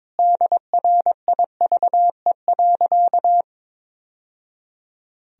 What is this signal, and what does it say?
Morse code, heard as a keyed tone, about 22 wpm, 700 Hz. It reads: DRIVE.
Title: Sentences from Top 1000 Words - Review Part 2 - 22wpm